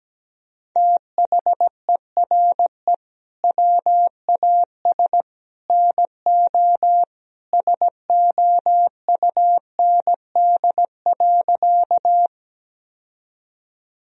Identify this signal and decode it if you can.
Morse code, heard as a keyed tone, about 17 wpm, 700 Hz. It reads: THERE WAS NO SOUND.
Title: Sentences from Top 1000 Words - Review Part 1 - 17wpm